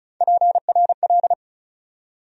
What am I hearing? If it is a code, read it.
Morse code, heard as a keyed tone, about 34 wpm, 700 Hz. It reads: PRL
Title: USA Call Sign Postfixes - 3 Letters 35wpm